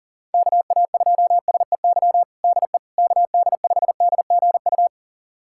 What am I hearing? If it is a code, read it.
Morse code, heard as a keyed tone, about 40 wpm, 700 Hz. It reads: KA2SEY DE KD5DGU
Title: USA Callsign DE Callsign 40wpm